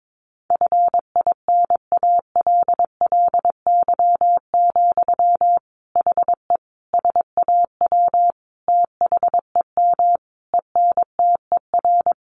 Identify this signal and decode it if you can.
Morse code, heard as a keyed tone, about 22 wpm, 700 Hz. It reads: FINALLY, HE SAW THEM ENTER
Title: Sentences from Top 1300 Words 22wpm